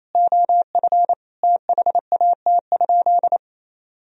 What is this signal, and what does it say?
Morse code, heard as a keyed tone, about 28 wpm, 700 Hz. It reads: OF THAT?
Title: Sentences from Top 100 Words and Abbreviations 28wpm